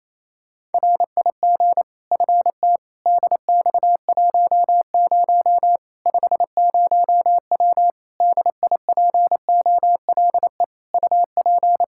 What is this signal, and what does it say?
Morse code, heard as a keyed tone, about 27 wpm, 700 Hz. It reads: RIG FT DX10 50W DIPOLE UP
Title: K1USN SST QSO Practice 28wpm